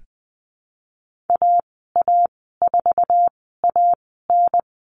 Morse code, heard as a keyed tone, about 20 wpm, spaced as for 15 wpm, 700 Hz. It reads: AA4AN
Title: 048 Lesson 9g - Number 4 - Call Signs 20wpm